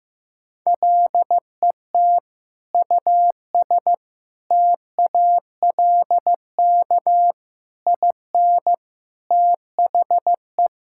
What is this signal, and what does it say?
Morse code, heard as a keyed tone, about 15 wpm, 700 Hz. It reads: LET US TALK IN THE
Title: Sentences from Top 1600 Words 15wpm